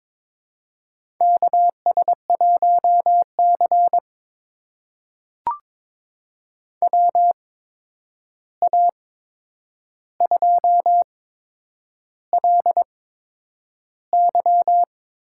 Morse code, heard as a keyed tone, about 22 wpm, spaced as for 9 wpm, 700 Hz. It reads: KS1C WA2LY
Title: USA Call Signs - 8x Character Spacing - 22wpm